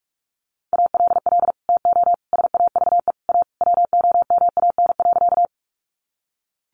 Morse code, heard as a keyed tone, about 44 wpm, 700 Hz. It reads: ALL TO SAVE A WOMAN.
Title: Sentences from Top 1100 Words 45wpm